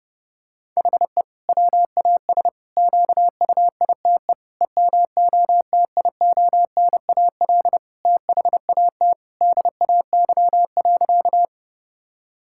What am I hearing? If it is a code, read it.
Morse code, heard as a keyed tone, about 30 wpm, 700 Hz. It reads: HE WAS QUITE EMOTIONAL THAT DAY.
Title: Sentences from Top 1000 Words - Review Part 5 - 30wpm